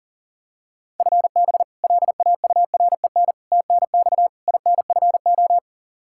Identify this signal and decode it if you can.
Morse code, heard as a keyed tone, about 40 wpm, 700 Hz. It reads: FB LAUREN TNX INFO
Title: K1USN SST QSO Practice 40wpm